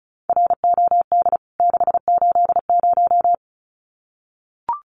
Morse code, heard as a keyed tone, about 34 wpm, 700 Hz. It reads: ROD 680
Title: CWT Exchange 35wpm